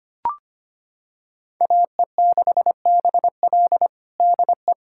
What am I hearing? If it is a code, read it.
Morse code, heard as a keyed tone, about 25 wpm, 700 Hz. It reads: AE6BL DE
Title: USA Callsign DE Callsign 25wpm